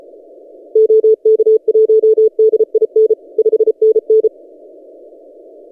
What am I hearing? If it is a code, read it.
Morse code, heard as a keyed tone, about 33 wpm, 440 Hz. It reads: OK1DIN 5NN
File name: tx6g-austral-is-12cw